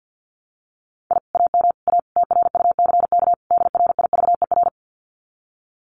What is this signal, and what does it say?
Morse code, heard as a keyed tone, about 50 wpm, 700 Hz. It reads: I AM A TRUCK DRIVER
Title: Common QSO Phrases Encore 50wpm